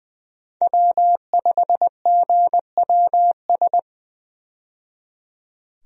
Morse code, heard as a keyed tone, about 20 wpm, 700 Hz. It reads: W5GWS
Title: USA Call Signs - 5 Characters - 20wpm